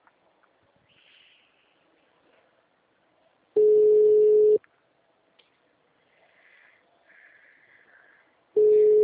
Ringback tone (Europe, 425 Hz).